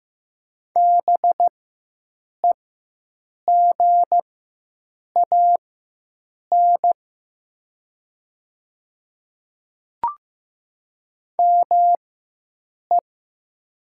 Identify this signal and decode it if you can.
Morse code, heard as a keyed tone, about 15 wpm, spaced as for 7 wpm, 700 Hz. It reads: BEGAN ME
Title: 4x Character Spacing - Top 500 Words - 15wpm